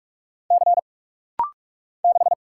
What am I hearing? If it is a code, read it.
Morse code, heard as a keyed tone, about 46 wpm, 700 Hz. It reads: C 6
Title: Single Letter-Number - Rapid-Fire 45wpm